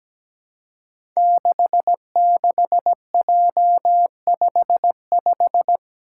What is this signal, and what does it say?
Morse code, heard as a keyed tone, about 17 wpm, 700 Hz. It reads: 66J55
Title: Five Challenging Letter-Number Combinations 17wpm